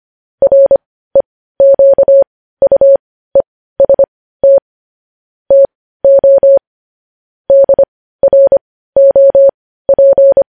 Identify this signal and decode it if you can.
Morse code, heard as a keyed tone, about 25 wpm, spaced as for 15 wpm, 550 Hz. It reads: REQUEST TO DROP
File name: podcast / News headlines-1x-25x15